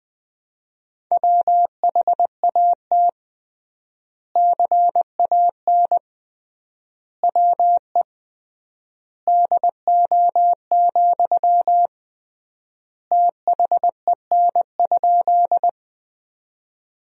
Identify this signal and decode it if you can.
Morse code, heard as a keyed tone, about 21 wpm, 700 Hz. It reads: WHAT CAN WE DO, THEN?
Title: 3x Word-Spacing - Sentences from Top 100 Words - 20wpm